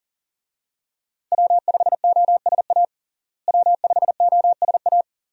Morse code, heard as a keyed tone, about 40 wpm, 700 Hz. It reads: W5OSA W5OSA
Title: USA Call Signs - 5 Characters - Repeated 2x - 40wpm